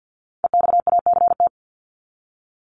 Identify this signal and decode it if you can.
Morse code, heard as a keyed tone, about 50 wpm, 700 Hz. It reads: EXACT
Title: Top 1251-1500 Words 50wpm